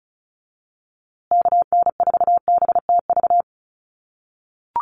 Morse code, heard as a keyed tone, about 34 wpm, 700 Hz. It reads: KN4BTV
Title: USA Call Signs Repeated 2x 35wpm